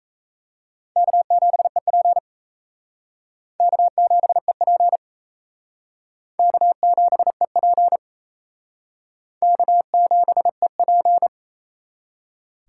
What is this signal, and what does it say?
Morse code, heard as a keyed tone, about 35 wpm, 700 Hz. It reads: K7EP K7EP K7EP K7EP
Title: Speed Racing - USA Call Signs 28wpm